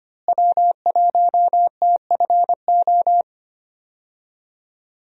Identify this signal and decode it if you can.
Morse code, heard as a keyed tone, about 25 wpm, 700 Hz. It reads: W1TFO
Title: USA Call Signs - 5 Characters - Repeated 2x - 25wpm